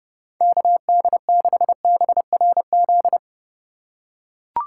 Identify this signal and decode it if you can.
Morse code, heard as a keyed tone, about 31 wpm, 700 Hz. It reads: KD6BRZ